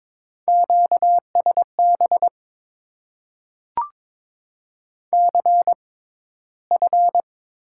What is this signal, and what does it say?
Morse code, heard as a keyed tone, about 22 wpm, spaced as for 13 wpm, 700 Hz. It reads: QSB CF
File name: Top 90 QSO Elements - 6x Character Spacing - 22wpm